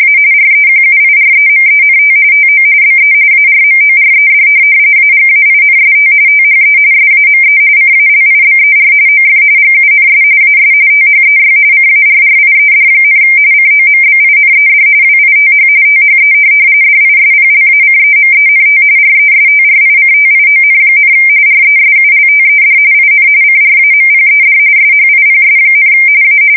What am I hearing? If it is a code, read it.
RTTY, about 45.45 baud. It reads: A REQUIREMENT TO BE ARMORED AGAINST THE SAME CALIBER OF GUNS AS THEY CARRIED, COMBINED WITH THE DISPLACEMENT RESTRICTION, RESULTED IN CRAMPED SHIPS, A